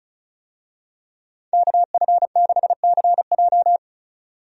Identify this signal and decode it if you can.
Morse code, heard as a keyed tone, about 34 wpm, 700 Hz. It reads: KF6CJ
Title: USA Call Signs - 5 Characters - 35wpm